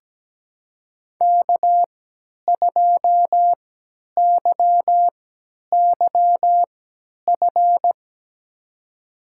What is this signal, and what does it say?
Morse code, heard as a keyed tone, about 17 wpm, spaced as for 10 wpm, 700 Hz. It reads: K2YYF
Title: USA Call Signs - 5 Characters - 3x Character Spacing - 17wpm